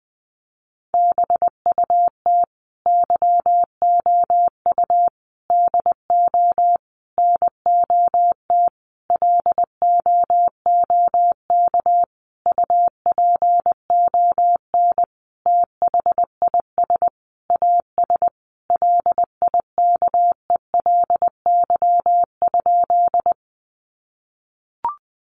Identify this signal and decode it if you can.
Morse code, heard as a keyed tone, about 20 wpm, 700 Hz. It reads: BUT YOU DO NOT LOOK UPON THIS AS LIKELY?
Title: Sentences from Top 1100 Words 20wpm